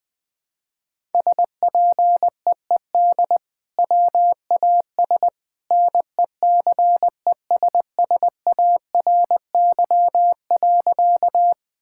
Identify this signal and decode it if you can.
Morse code, heard as a keyed tone, about 20 wpm, 700 Hz. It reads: SPEED WAS NECESSARY.
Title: Sentences from Top 1000 Words - Review Part 5 - 20wpm